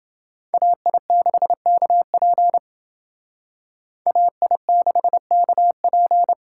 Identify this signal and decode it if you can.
Morse code, heard as a keyed tone, about 28 wpm, 700 Hz. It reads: AI6KP AI6KP